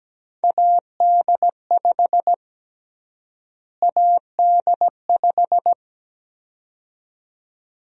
Morse code, heard as a keyed tone, about 17 wpm, 700 Hz. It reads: AD5 AD5
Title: USA Call Sign Prefixes - Repeated 2x - 17wpm